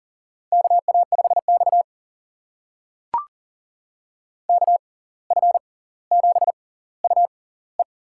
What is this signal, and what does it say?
Morse code, heard as a keyed tone, about 40 wpm, spaced as for 19 wpm, 700 Hz. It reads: KA5X KF7UE
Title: USA Call Signs - 6x Character Spacing - 40wpm